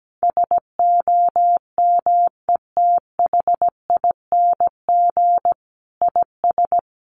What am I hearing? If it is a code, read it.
Morse code, heard as a keyed tone, about 17 wpm, 700 Hz. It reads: SOMETHING IS
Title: Sentences from Top 1400 Words 17wpm